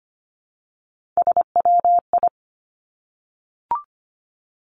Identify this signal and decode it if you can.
Morse code, heard as a keyed tone, about 25 wpm, 700 Hz. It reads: SWI
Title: Three Challenging Letter-Number Combinations 25wpm